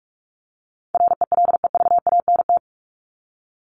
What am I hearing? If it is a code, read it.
Morse code, heard as a keyed tone, about 44 wpm, 700 Hz. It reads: RELEVANT
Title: Top 1001-1250 Words 45wpm